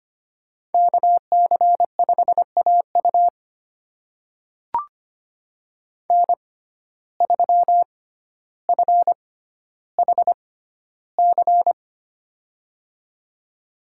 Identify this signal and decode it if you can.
Morse code, heard as a keyed tone, about 25 wpm, spaced as for 13 wpm, 700 Hz. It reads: KC5AU N3FHC
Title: USA Call Signs - 5 Characters - 6x Character Spacing - 25wpm